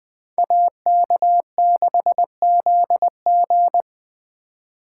Morse code, heard as a keyed tone, about 23 wpm, 700 Hz. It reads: AK6ZG
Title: Speed Racing - USA 2x2 Call Signs 20wpm